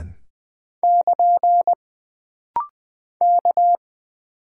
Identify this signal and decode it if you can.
Morse code, heard as a keyed tone, about 20 wpm, 700 Hz. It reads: <KN> K
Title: Single Character (Letter:Number:Punctuation:Prosign) - Rapid-Fire - 20wpm